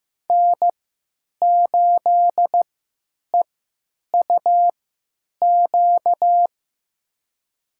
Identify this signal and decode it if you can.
Morse code, heard as a keyed tone, about 15 wpm, spaced as for 9 wpm, 700 Hz. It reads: N8EUQ
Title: USA Call Signs - 5 Characters - 3x Character Spacing - 15wpm